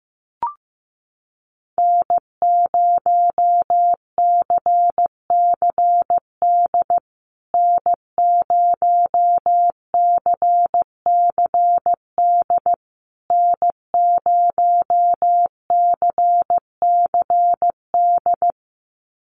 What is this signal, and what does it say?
Morse code, heard as a keyed tone, about 15 wpm, 700 Hz. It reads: N0CCD N0CCD N0CCD
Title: USA Call Signs Repeated 3x 15wpm